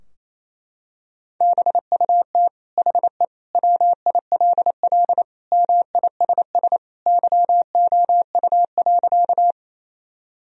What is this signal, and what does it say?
Morse code, heard as a keyed tone, about 28 wpm, 700 Hz. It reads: BUT HE WILL MISS YOU.
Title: Sentences from Top 1000 Words - Review Part 3 - 28wpm